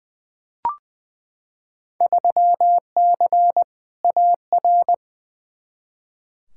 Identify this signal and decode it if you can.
Morse code, heard as a keyed tone, about 21 wpm, 700 Hz. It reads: 3C AR
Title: Field Day Exchanges Encore - 20wpm